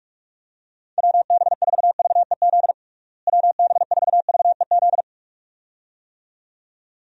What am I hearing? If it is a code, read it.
Morse code, heard as a keyed tone, about 46 wpm, 700 Hz. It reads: WB4VEZ WB4VEZ
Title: USA Call Signs - 6 Characters - Repeated 2x - 45wpm